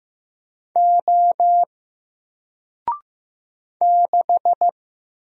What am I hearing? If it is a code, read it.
Morse code, heard as a keyed tone, about 15 wpm, 700 Hz. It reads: O 6
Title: Single Character (Letter:Number:Punctuation:Prosign) Alt 15wpm